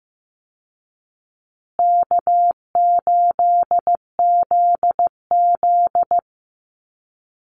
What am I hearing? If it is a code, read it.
Morse code, heard as a keyed tone, about 14 wpm, 700 Hz. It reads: K8ZZ
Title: Easy Call Signs Repeated 3x - 15wpm